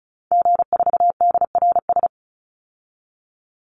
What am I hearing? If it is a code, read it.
Morse code, heard as a keyed tone, about 34 wpm, 700 Hz. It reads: G4DRS
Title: Call Signs - 35wpm